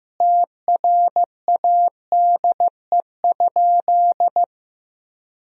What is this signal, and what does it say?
Morse code, heard as a keyed tone, about 15 wpm, 700 Hz. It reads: TRADE?
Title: Sentences from Top 1000 Words - Review Part 4 - 15wpm